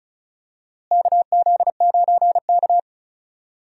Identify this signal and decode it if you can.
Morse code, heard as a keyed tone, about 34 wpm, 700 Hz. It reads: KZ9K
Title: USA Call Signs - 4 Characters - Repeated 3x - 35wpm